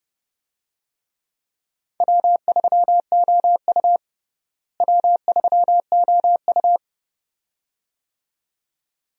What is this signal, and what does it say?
Morse code, heard as a keyed tone, about 32 wpm, 700 Hz. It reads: W3OU W3OU
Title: USA Call Signs - 4 Characters - Repeated 2x - 30wpm